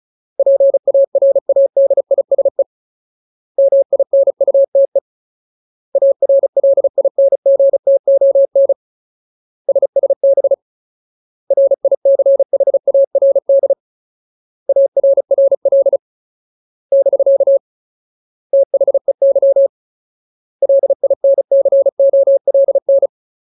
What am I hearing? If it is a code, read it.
Morse code, heard as a keyed tone, about 34 wpm, 550 Hz. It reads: PARADISE MINUTE ARLINGTON SSB RICHARD ARRL <BK> THEY LINCOLN
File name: Ultimate Word List / 35WPM Ultimate